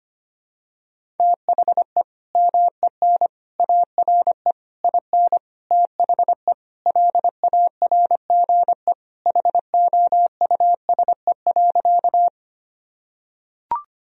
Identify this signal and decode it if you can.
Morse code, heard as a keyed tone, about 25 wpm, 700 Hz. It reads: THE MEN ARE IN THE LARGE HOUSE.
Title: Sentences from Top 1000 Words - Review Part 1 - 25wpm